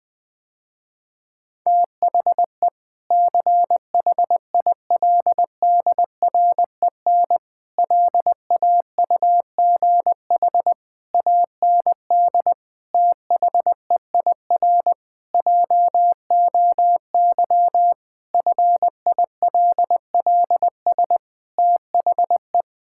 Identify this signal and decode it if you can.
Morse code, heard as a keyed tone, about 20 wpm, 700 Hz. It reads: THE CHILDREN LAUGH AND THEIR JOY FILLS THE
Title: Sentences with 2-Word Phrases - Spoken Beginning - 20wpm